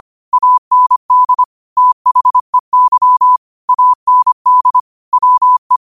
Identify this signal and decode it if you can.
Morse code, heard as a keyed tone, about 25 wpm, 1000 Hz. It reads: AND THEY AND WE
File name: qotdfiles / 25wpm